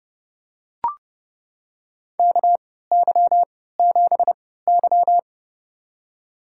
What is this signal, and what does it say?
Morse code, heard as a keyed tone, about 32 wpm, spaced as for 17 wpm, 700 Hz. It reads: KY7Y
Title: USA Call Signs - 4 Characters - 3x Character Spacing - 30wpm